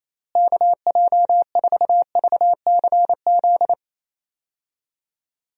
Morse code, heard as a keyed tone, about 28 wpm, 700 Hz. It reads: KJ4VCZ
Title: USA Call Signs - 6 Characters - Repeated 2x - 28wpm